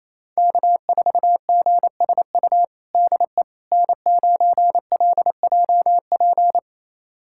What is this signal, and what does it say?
Morse code, heard as a keyed tone, about 28 wpm, 700 Hz. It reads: K4GSU DE N9LJP
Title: USA Callsign DE Callsign 28wpm